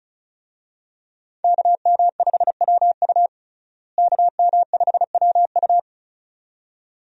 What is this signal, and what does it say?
Morse code, heard as a keyed tone, about 36 wpm, 700 Hz. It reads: KM5WU KM5WU
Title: USA Call Signs - 5 Characters - Repeated 2x - 35wpm